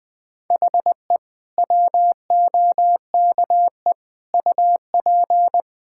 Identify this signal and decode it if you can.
Morse code, heard as a keyed tone, about 21 wpm, 700 Hz. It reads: HE WOKE UP
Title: Sentences with 4-Word Phrases - Spoken Beginning - 20wpm